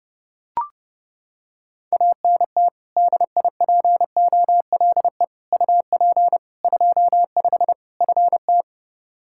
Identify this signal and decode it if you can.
Morse code, heard as a keyed tone, about 30 wpm, 700 Hz. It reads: ANT DIPOLE UP 25 FT
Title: K1USN SST QSO Practice 30wpm